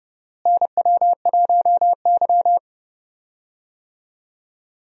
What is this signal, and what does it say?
Morse code, heard as a keyed tone, about 31 wpm, 700 Hz. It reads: NW1Y
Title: USA Call Signs - 4 Characters - Repeated 2x - 30wpm